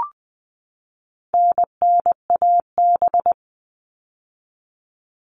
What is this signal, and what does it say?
Morse code, heard as a keyed tone, about 20 wpm, 700 Hz. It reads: NNAB
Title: All Cut Numbers - 4-Digit 20wpm